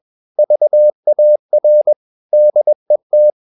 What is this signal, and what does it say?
Morse code, heard as a keyed tone, about 21 wpm, 600 Hz. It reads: VAR DET
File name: Ferie QRS 21wpm